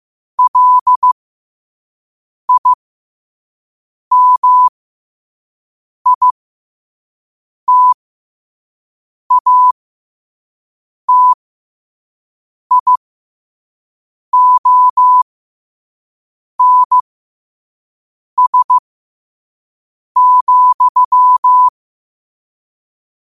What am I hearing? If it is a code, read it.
Morse code, heard as a keyed tone, about 15 wpm, spaced as for 5 wpm, 1000 Hz. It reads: LIMITATIONS,
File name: 5wpm